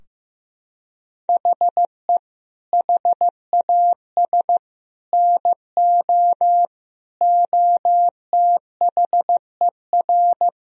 Morse code, heard as a keyed tone, about 16 wpm, 700 Hz. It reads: HE HAS NO OTHER
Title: Sentences from Top 1000 Words - Review Part 4 - 15wpm